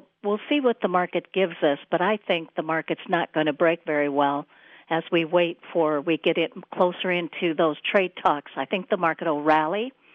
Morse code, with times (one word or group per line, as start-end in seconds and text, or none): none